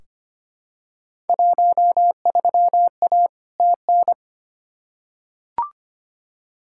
Morse code, heard as a keyed tone, about 25 wpm, 700 Hz. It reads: 13A TN
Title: Field Day Exchanges Encore - 25wpm